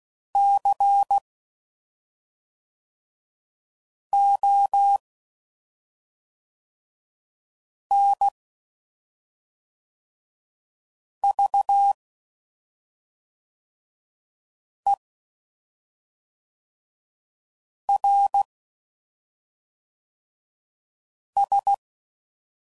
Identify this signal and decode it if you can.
Morse code, heard as a keyed tone, about 16 wpm, 800 Hz. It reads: C O N V E R S
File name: Suffixes / Special ING's / SlowToFaster ING's